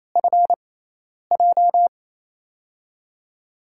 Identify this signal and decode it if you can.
Morse code, heard as a keyed tone, about 28 wpm, spaced as for 10 wpm, 700 Hz. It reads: FJ
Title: USA Call Signs - 6x Character Spacing - 28wpm